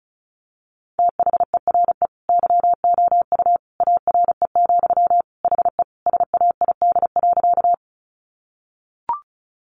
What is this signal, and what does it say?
Morse code, heard as a keyed tone, about 34 wpm, 700 Hz. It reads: THERE YOU ARE, HE SAID.
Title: Sentences from Top 100 Words 35wpm